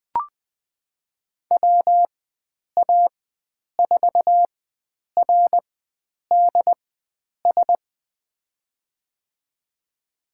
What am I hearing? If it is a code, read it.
Morse code, heard as a keyed tone, about 20 wpm, spaced as for 9 wpm, 700 Hz. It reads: WA4RDS
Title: USA Call Signs - 6 Characters - 4x Character Spacing - 20wpm